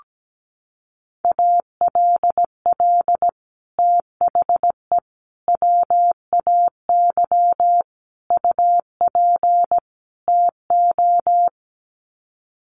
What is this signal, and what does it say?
Morse code, heard as a keyed tone, about 17 wpm, 700 Hz. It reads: ALL THE WAY UP TO
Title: Sets of 5 Words Encore 17wpm